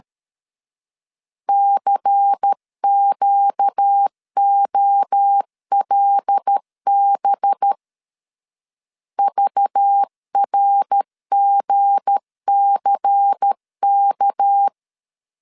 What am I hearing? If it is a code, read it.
Morse code, heard as a keyed tone, about 13 wpm, 800 Hz. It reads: CQOLB VRGCK
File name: LEZIONI / 7 GF9Z / 7 G 12wpm 2055